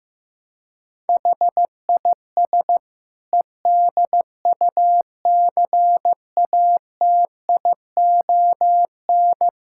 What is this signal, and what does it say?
Morse code, heard as a keyed tone, about 15 wpm, 700 Hz. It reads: HIS EDUCATION
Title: Sentences from Top 1000 Words - Review Part 6 - 15wpm